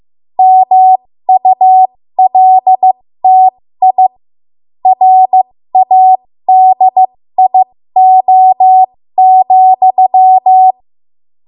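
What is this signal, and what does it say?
Morse code, heard as a keyed tone, about 15 wpm, spaced as for 13 wpm, 750 Hz. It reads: MULTI RADIO,
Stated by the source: 13 wpm Morse code practice archive transmission.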